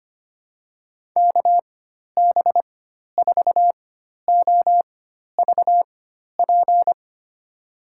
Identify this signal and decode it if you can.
Morse code, heard as a keyed tone, about 25 wpm, spaced as for 12 wpm, 700 Hz. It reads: KB4OVP
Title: USA Call Signs - 6 Characters - 4x Character Spacing - 25wpm